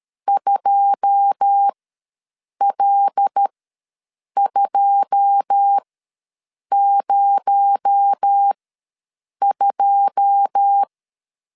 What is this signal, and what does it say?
Morse code, heard as a keyed tone, about 13 wpm, spaced as for 7 wpm, 800 Hz. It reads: 2L202
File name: LEZIONI / 9 2 ws 12wpm 3406